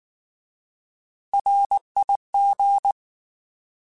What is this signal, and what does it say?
Morse code, heard as a keyed tone, about 19 wpm, 800 Hz. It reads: RIG
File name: Alphabetical / Alphabetical mp3s / Rs @18